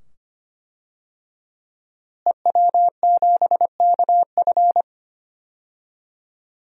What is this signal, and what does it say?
Morse code, heard as a keyed tone, about 25 wpm, 700 Hz. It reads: EW7KF
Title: DXCC-100 Call Signs - 5 Characters - Repeated 2x - 25wpm